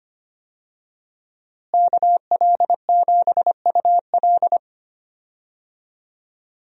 Morse code, heard as a keyed tone, about 25 wpm, 700 Hz. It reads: KL7UL
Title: USA Call Signs - 5 Characters - Repeated 3x - 25wpm